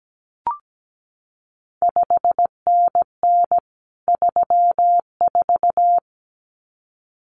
Morse code, heard as a keyed tone, about 17 wpm, 700 Hz. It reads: 5NN 34
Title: CQ WW DX Contest Exchange Encore - 17wpm